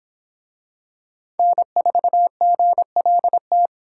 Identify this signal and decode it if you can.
Morse code, heard as a keyed tone, about 24 wpm, 700 Hz. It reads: N4GLT